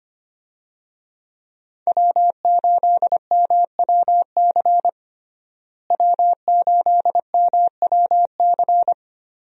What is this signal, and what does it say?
Morse code, heard as a keyed tone, about 25 wpm, 700 Hz. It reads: W8MWC W8MWC
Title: USA Call Signs - 5 Characters - Repeated 2x - 25wpm